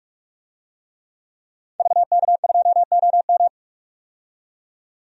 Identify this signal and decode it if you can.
Morse code, heard as a keyed tone, about 46 wpm, 700 Hz. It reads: VK2OM
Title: DXCC-100 Call Signs - 5 Characters - 45wpm